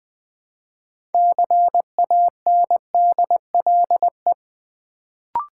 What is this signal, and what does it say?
Morse code, heard as a keyed tone, about 20 wpm, 700 Hz. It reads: CANDLE
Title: Build A Word - 3 Letters At A Time - 6-Letter Words - 20wpm